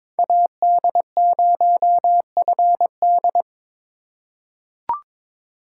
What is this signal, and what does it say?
Morse code, heard as a keyed tone, about 22 wpm, 700 Hz. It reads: AD0FD
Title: USA Call Signs - 5 Characters - 3x Character Spacing - 22wpm